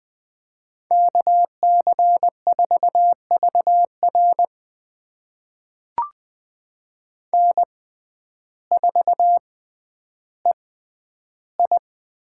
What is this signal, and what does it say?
Morse code, heard as a keyed tone, about 20 wpm, spaced as for 11 wpm, 700 Hz. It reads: KC4VR N4EI
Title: USA Call Signs - 5 Characters - 6x Character Spacing - 20wpm